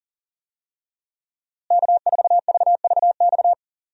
Morse code, heard as a keyed tone, about 40 wpm, 700 Hz. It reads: K4VVX
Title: USA Call Signs - 5 Characters - Repeated 3x - 40wpm